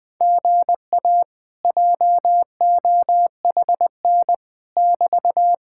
Morse code, heard as a keyed tone, about 20 wpm, 700 Hz. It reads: GA JOHN <BT>
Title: Speed-Racing Common QSO Phrases - 20wpm